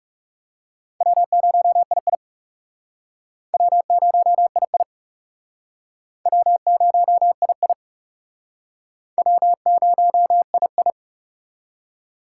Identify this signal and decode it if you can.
Morse code, heard as a keyed tone, about 36 wpm, 700 Hz. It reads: W0II W0II W0II W0II
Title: Speed Racing - USA 1x2 Call Signs 30wpm